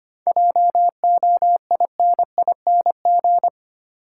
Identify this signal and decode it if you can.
Morse code, heard as a keyed tone, about 25 wpm, 700 Hz. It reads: JOINING
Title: Sets of 5 Words - New Word Drill - 25wpm